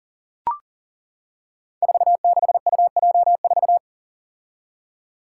Morse code, heard as a keyed tone, about 40 wpm, 700 Hz. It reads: 46UJ4
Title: Five Challenging Letter-Number Combinations 40wpm